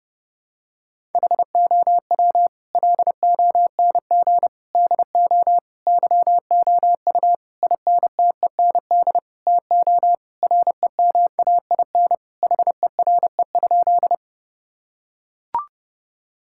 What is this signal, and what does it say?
Morse code, heard as a keyed tone, about 30 wpm, 700 Hz. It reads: HOW LONG DO YOU INTEND TO REMAIN HERE?
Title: Sentences from Top 1600 Words 30wpm